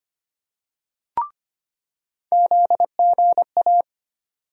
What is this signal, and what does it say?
Morse code, heard as a keyed tone, about 25 wpm, 700 Hz. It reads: ZGA
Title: USA Call Sign Postfixes - 3 Letters 25wpm